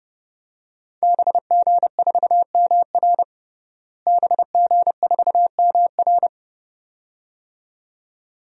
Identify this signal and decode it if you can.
Morse code, heard as a keyed tone, about 32 wpm, 700 Hz. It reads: BG4MR BG4MR
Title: DXCC-100 Call Signs - 5 Characters - Repeated 2x - 30wpm